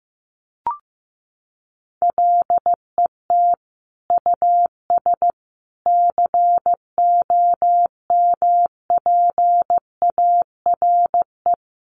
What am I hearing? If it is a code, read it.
Morse code, heard as a keyed tone, about 15 wpm, 700 Hz. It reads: LET US COMPARE
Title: Sentences from Top 1300 Words 15wpm